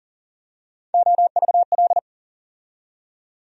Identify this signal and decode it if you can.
Morse code, heard as a keyed tone, about 40 wpm, 700 Hz. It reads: OVL
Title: USA Call Sign Postfixes - 3 Letters 40wpm